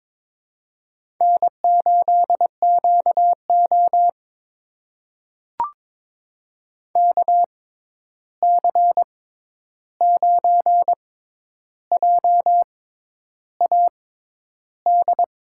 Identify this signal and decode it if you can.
Morse code, heard as a keyed tone, about 22 wpm, spaced as for 10 wpm, 700 Hz. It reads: N8QO KC9JAD
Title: USA Call Signs - 6x Character Spacing - 22wpm